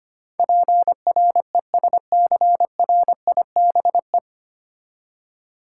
Morse code, heard as a keyed tone, about 25 wpm, 700 Hz. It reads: PRESCRIBE
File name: Nine Letter Words - 25wpm